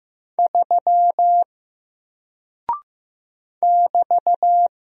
Morse code, heard as a keyed tone, about 15 wpm, 700 Hz. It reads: 3 <BT>
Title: Single Character (Letter:Number:Punctuation:Prosign) Alt 15wpm